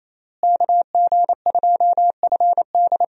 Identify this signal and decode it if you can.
Morse code, heard as a keyed tone, about 28 wpm, 700 Hz. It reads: KG2FD
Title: USA Call Signs - 5 Characters - Repeated 2x - 28wpm